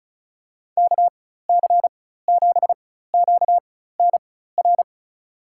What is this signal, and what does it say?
Morse code, heard as a keyed tone, about 34 wpm, spaced as for 16 wpm, 700 Hz. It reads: KC7QNR
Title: USA Call Signs - 6 Characters - 4x Character Spacing - 35wpm